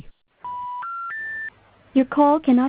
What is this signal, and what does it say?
Special information tone, with speech.